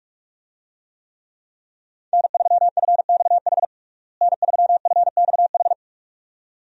Morse code, heard as a keyed tone, about 46 wpm, 700 Hz. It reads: N3FXH N3FXH
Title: USA Call Signs - 5 Characters - Repeated 2x - 45wpm